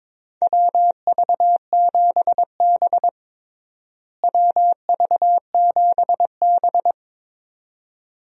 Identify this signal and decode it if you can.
Morse code, heard as a keyed tone, about 22 wpm, 700 Hz. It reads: WV7B WV7B
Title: USA Call Signs - 4 Characters - Repeated 2x - 22wpm